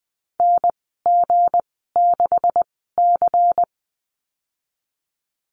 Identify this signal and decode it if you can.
Morse code, heard as a keyed tone, about 20 wpm, spaced as for 15 wpm, 700 Hz. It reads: NG6C
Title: 173 Lesson 28g - Number 9 - Call Signs 20wpm